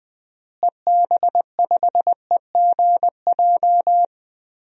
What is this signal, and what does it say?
Morse code, heard as a keyed tone, about 20 wpm, 700 Hz. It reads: EB5EGJ
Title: DXCC-100 Call Signs - 6 Characters - Repeated 2x - 20wpm